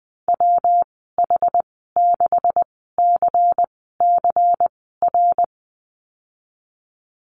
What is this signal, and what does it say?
Morse code, heard as a keyed tone, about 20 wpm, spaced as for 15 wpm, 700 Hz. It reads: WH6CCR
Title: 173 Lesson 28g - Number 9 - Call Signs 20wpm